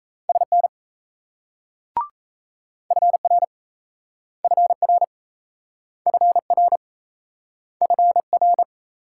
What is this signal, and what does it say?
Morse code, heard as a keyed tone, about 36 wpm, 700 Hz. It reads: SN FR FR FR FR